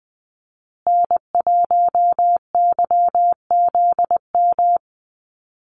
Morse code, heard as a keyed tone, about 20 wpm, 700 Hz. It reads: N1YZM
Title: USA Call Signs Repeated 3x 20wpm